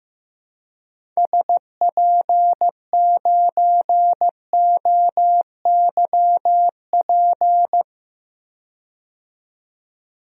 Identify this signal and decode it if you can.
Morse code, heard as a keyed tone, about 15 wpm, 700 Hz. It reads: SP9OYP
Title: DXCC-100 Call Signs - 6 Characters - 15wpm